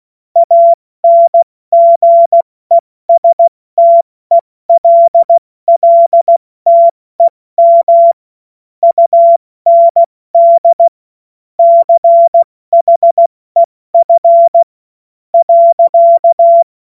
Morse code, heard as a keyed tone, about 16 wpm, spaced as for 14 wpm, 680 Hz. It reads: ANGESTELLTEM UND CHEF.
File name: Morsekurs / Witze / Witz 04. April 70 F80